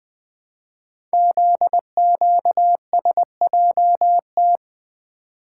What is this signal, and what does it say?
Morse code, heard as a keyed tone, about 20 wpm, 700 Hz. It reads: ZQSJT
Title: Code Groups - 5 Letters - 20wpm